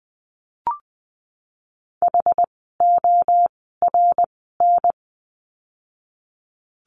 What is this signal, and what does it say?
Morse code, heard as a keyed tone, about 20 wpm, 700 Hz. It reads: H O R N
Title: 061 Lesson 11f - Letter h - Words 20wpm